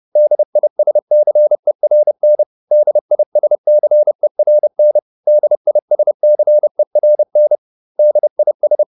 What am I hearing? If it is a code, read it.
Morse code, heard as a keyed tone, about 30 wpm, 600 Hz. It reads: DISCERN DISCERN DISCERN DIS